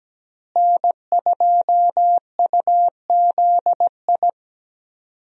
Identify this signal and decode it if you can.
Morse code, heard as a keyed tone, about 17 wpm, 700 Hz. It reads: N2UZI
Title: USA Call Signs - 5 Characters - Repeated 2x - 17wpm